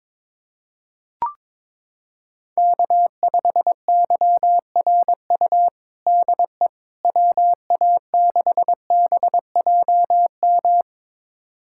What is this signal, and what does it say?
Morse code, heard as a keyed tone, about 21 wpm, 700 Hz. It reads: K5YRU DE WA6BJM
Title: USA Callsign DE Callsign 22wpm